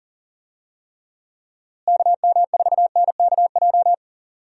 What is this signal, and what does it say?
Morse code, heard as a keyed tone, about 40 wpm, 700 Hz. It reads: KM4NKJ
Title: USA Call Signs - 6 Characters - Repeated 2x - 40wpm